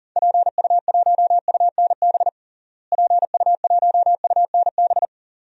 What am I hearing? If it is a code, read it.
Morse code, heard as a keyed tone, about 37 wpm, 700 Hz. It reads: PU1UNB PU1UNB
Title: DXCC-100 Call Signs - 6 Characters - Repeated 2x - 40wpm